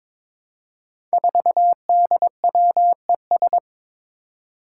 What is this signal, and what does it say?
Morse code, heard as a keyed tone, about 22 wpm, 700 Hz. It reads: 4DWES
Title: Five Challenging Letter-Number Combinations 22wpm